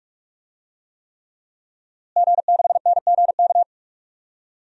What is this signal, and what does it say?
Morse code, heard as a keyed tone, about 47 wpm, 700 Hz. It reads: G6NGK
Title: DXCC-100 Call Signs - 5 Characters - Repeated 3x - 45wpm